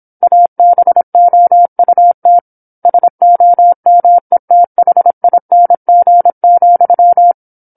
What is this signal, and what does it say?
Morse code, heard as a keyed tone, about 27 wpm, 700 Hz. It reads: ABOUT SOMETHING,